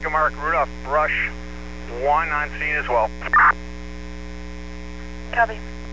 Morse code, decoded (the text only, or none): none